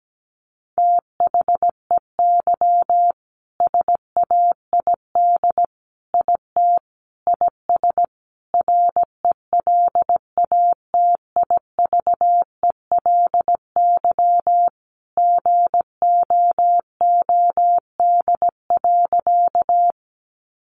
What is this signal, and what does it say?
Morse code, heard as a keyed tone, about 17 wpm, 700 Hz. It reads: THEY SAID IT IS RELATIVELY GOOD.
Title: Sentences from Top 1100 Words 17wpm